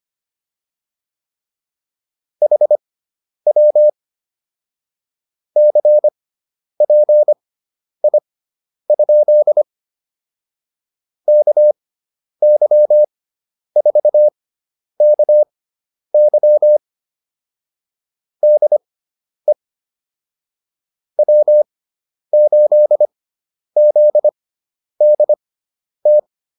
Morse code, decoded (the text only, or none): HW CPI? KY4KY DE W8ZDT